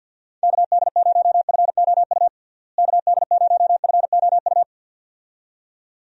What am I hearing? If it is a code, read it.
Morse code, heard as a keyed tone, about 52 wpm, 700 Hz. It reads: KD0FOU KD0FOU
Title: USA Call Signs - 6 Characters - Repeated 2x - 50wpm